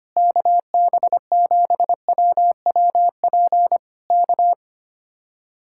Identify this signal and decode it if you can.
Morse code, heard as a keyed tone, about 24 wpm, 700 Hz. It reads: KB7WWP K
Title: Speed-Racing Common QSO Phrases - 25wpm